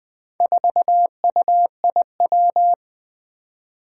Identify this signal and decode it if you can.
Morse code, heard as a keyed tone, about 20 wpm, 700 Hz. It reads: 4UIW
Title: Four Challenging Letter-Number Combinations 20wpm